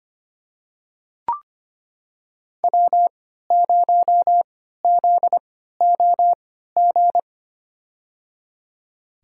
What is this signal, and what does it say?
Morse code, heard as a keyed tone, about 25 wpm, spaced as for 14 wpm, 700 Hz. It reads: W0ZOG
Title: USA Call Signs - 5 Characters - 3x Character Spacing - 25wpm